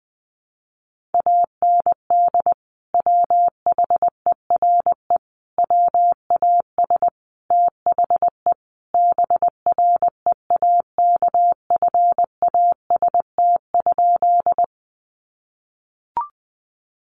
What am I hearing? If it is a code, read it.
Morse code, heard as a keyed tone, about 20 wpm, 700 Hz. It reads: AND WHERE WAS THE BREAKFAST?
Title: Sentences from Top 1500 Words 20wpm